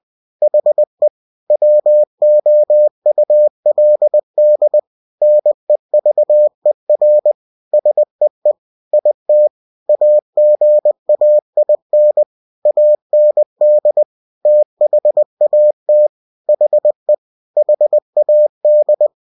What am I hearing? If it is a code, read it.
Morse code, heard as a keyed tone, about 20 wpm, 600 Hz. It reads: HE WOULD NEVER SEE IT AGAIN AND THAT HE HAD
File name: cw-books / winnie-the-pooh / winnie-20wpm-page-0093